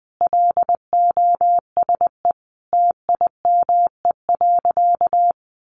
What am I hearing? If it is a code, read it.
Morse code, heard as a keyed tone, about 20 wpm, 700 Hz. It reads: LOSE TIME.
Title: Sentences from Top 1100 Words 20wpm